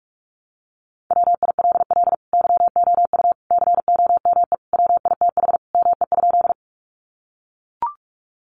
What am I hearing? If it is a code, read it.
Morse code, heard as a keyed tone, about 44 wpm, 700 Hz. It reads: WILL YOU COME WITH ME?
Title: Sentences from Top 200 Words 45wpm